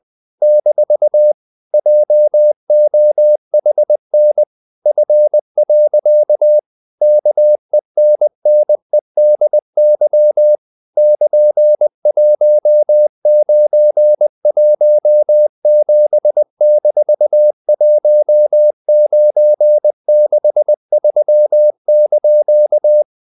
Morse code, heard as a keyed tone, about 20 wpm, 600 Hz. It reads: - JOHN F. KENNEDY <KN>1917-1963)
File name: CW / Practice / EnglishQuotations0320